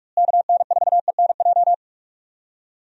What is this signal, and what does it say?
Morse code, heard as a keyed tone, about 46 wpm, 700 Hz. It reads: KN4ENJ
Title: USA Call Signs - 6 Characters - Repeated 2x - 45wpm